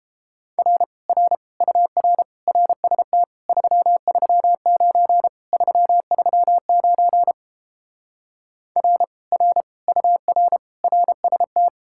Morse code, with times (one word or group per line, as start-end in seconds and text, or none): 0.6-0.8 R
1.1-1.4 R
1.6-2.2 UR
2.5-3.2 RST
3.5-5.3 339
5.5-7.3 339
8.8-9.0 R
9.3-9.6 R
9.9-10.6 UR
10.8-11.7 RST